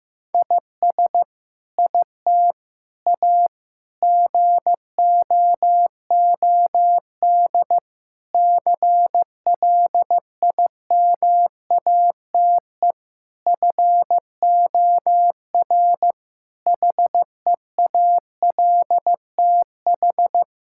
Morse code, heard as a keyed tone, about 16 wpm, 700 Hz. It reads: IS IT A GOOD CLIMATE FOR HEALTH
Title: Sentences from Top 1600 Words 15wpm